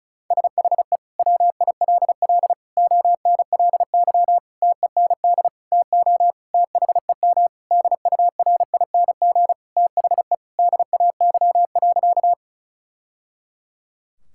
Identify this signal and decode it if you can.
Morse code, heard as a keyed tone, about 36 wpm, 700 Hz. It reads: SHE WILL ONLY TEND TO THEM DURING THE DAY.
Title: Sentences from Top 1000 Words - Review Part 5 - 35wpm